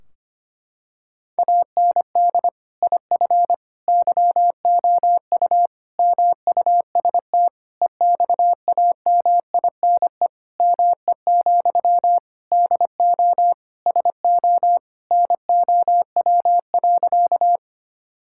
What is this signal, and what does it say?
Morse code, heard as a keyed tone, about 25 wpm, 700 Hz. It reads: AND IF YOU MUST EXAMINE ME, DO SO NOW.
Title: Sentences from Top 1600 Words 25wpm